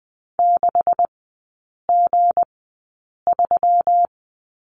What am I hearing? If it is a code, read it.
Morse code, heard as a keyed tone, about 19 wpm, spaced as for 8 wpm, 700 Hz. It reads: 6G3
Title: 159 Lesson 26e - Letter g - Three Characters 20wpm